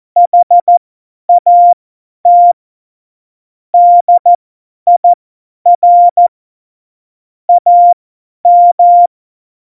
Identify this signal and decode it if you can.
Morse code, heard as a keyed tone, about 14 wpm, spaced as for 10 wpm, 700 Hz. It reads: HAT DIR AM